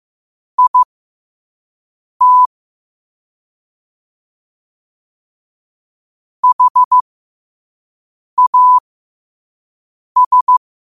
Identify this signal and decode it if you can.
Morse code, heard as a keyed tone, about 15 wpm, spaced as for 5 wpm, 1000 Hz. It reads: IT HAS
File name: qotdfiles / 5wpm